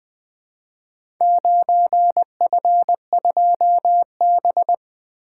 Morse code, heard as a keyed tone, about 20 wpm, 700 Hz. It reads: 9F2B
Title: Four Letter-Number Combinations - 20wpm